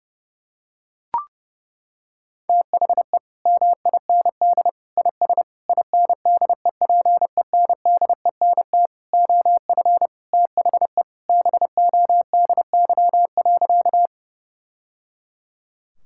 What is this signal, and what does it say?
Morse code, heard as a keyed tone, about 28 wpm, 700 Hz. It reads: THE MIND IS INDEPENDENT OF THE BODY.
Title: Sentences from Top 1000 Words - Review Part 6 - 30wpm